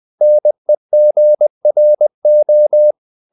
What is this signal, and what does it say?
Morse code, heard as a keyed tone, about 20 wpm, 600 Hz. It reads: NEGRO
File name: the old man and the sea part 2 20wpm